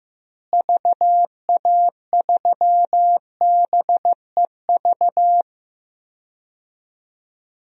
Morse code, heard as a keyed tone, about 15 wpm, 700 Hz. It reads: VA3BEV
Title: DXCC-100 Call Signs - 6 Characters - Repeated 3x - 15wpm